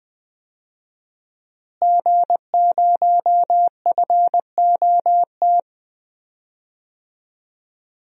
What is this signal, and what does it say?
Morse code, heard as a keyed tone, about 20 wpm, 700 Hz. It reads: G0FOT
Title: DXCC-100 Call Signs - 5 Characters - Repeated 3x - 20wpm